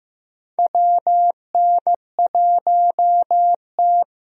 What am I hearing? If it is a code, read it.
Morse code, heard as a keyed tone, about 15 wpm, 700 Hz. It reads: WN1T
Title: USA Call Signs - 4 Characters - Repeated 3x - 15wpm